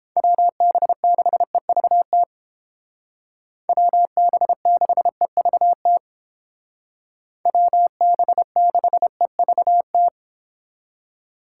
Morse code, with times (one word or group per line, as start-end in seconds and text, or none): 0.2-2.2 WB6EVT
3.7-6.0 WB6EVT
7.5-10.1 WB6EVT